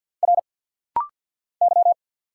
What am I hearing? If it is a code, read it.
Morse code, heard as a keyed tone, about 50 wpm, 690 Hz. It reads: R Y
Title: Single Letters - Rapid-Fire 50wpm